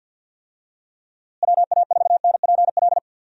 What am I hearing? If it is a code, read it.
Morse code, heard as a keyed tone, about 52 wpm, 700 Hz. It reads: WA4NPL
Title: USA Call Signs - 6 Characters - Repeated 3x - 50wpm